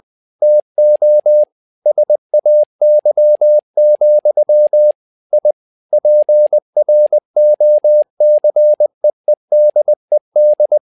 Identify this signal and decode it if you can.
Morse code, heard as a keyed tone, about 20 wpm, 600 Hz. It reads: TO SAY, I PROCEEDED
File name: tower trouble 20wpm